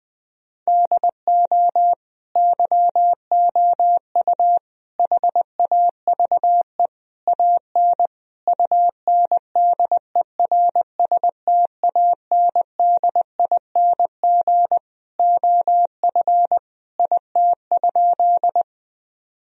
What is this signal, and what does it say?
Morse code, heard as a keyed tone, about 20 wpm, 700 Hz. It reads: DO YOU HAVE AN UNDERSTANDING OF IT?
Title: Sentences from Top 1000 Words - Review Part 3 - 20wpm